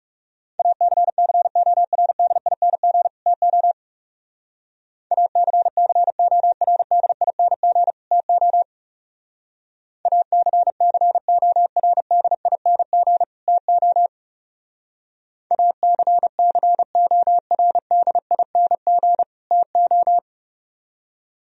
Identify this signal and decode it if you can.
Morse code, heard as a keyed tone, about 36 wpm, 700 Hz. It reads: ACCORDING TO ACCORDING TO ACCORDING TO ACCORDING TO
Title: Speed-Racing - Sets of 2 Words 30wpm